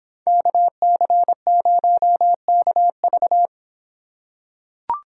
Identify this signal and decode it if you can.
Morse code, heard as a keyed tone, about 26 wpm, 700 Hz. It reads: KC0KV